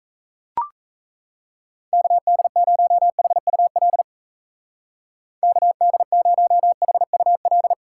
Morse code, heard as a keyed tone, about 34 wpm, 700 Hz. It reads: KD0HUL KD0HUL
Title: Speed Racing - USA 2x3 Call Signs 28wpm